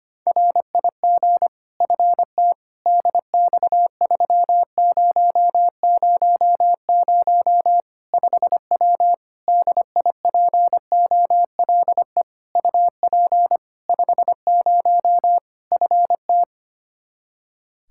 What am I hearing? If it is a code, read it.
Morse code, heard as a keyed tone, about 25 wpm, 700 Hz. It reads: RIG FT DX3000 5W DIPOLE UP 50 FT
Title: Practice QSO - Exchange 2 - Info - 25wpm